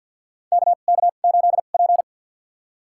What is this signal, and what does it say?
Morse code, heard as a keyed tone, about 47 wpm, spaced as for 35 wpm, 700 Hz. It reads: KK8P
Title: USA Call Signs - 4 Characters - 2x Character Spacing - 50wpm